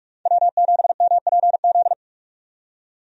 Morse code, heard as a keyed tone, about 42 wpm, 700 Hz. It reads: W7MPZ
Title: USA Call Signs - 5 Characters - 45wpm